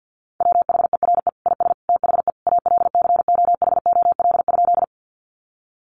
Morse code, heard as a keyed tone, about 50 wpm, 700 Hz. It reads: WHERE IS THE ALCOHOL?
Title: Sentences from Top 1400 Words 50wpm